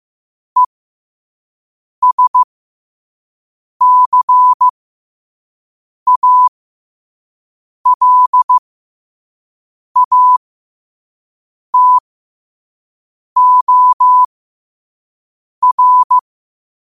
Morse code, heard as a keyed tone, about 15 wpm, spaced as for 5 wpm, 1000 Hz. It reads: ESCALATOR
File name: qotdfiles / 5wpm